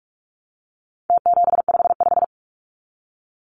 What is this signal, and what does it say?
Morse code, heard as a keyed tone, about 44 wpm, 700 Hz. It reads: T755
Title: Numbers - Contest Style - 4-Digit 45wpm